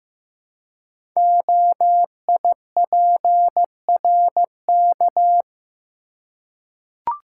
Morse code, heard as a keyed tone, about 15 wpm, 700 Hz. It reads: OIPRK
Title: Code Groups - 5 Letters - 15wpm